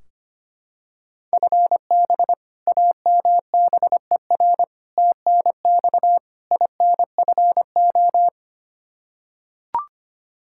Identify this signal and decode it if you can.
Morse code, heard as a keyed tone, about 25 wpm, 700 Hz. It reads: FB AMBER TNX INFO
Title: Practice QSO - Exchange 3 - More Info - 25wpm